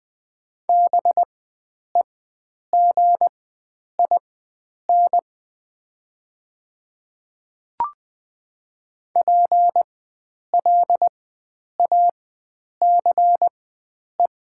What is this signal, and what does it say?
Morse code, heard as a keyed tone, about 20 wpm, spaced as for 9 wpm, 700 Hz. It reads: BEGIN PLACE
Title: 4x Character Spacing - Top 500 Words - 20wpm